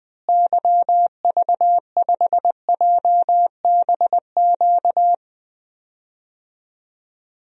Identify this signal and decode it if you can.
Morse code, heard as a keyed tone, about 21 wpm, 700 Hz. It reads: YV5JBQ
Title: DXCC-100 Call Signs - 6 Characters - Repeated 2x - 20wpm